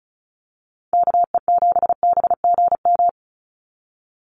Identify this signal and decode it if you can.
Morse code, heard as a keyed tone, about 36 wpm, 700 Hz. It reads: KE7BGM
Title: NA SOTA Activator Callsigns 35wpm